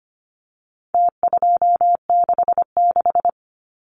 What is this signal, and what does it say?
Morse code, heard as a keyed tone, about 25 wpm, 700 Hz. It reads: T266
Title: Numbers - Contest Style - 4-Digit 25wpm